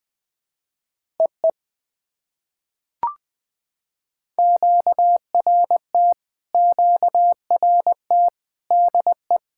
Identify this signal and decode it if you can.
Morse code, heard as a keyed tone, about 20 wpm, 700 Hz. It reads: EE QRT QRT DE
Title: POTA QSO Practice - 20wpm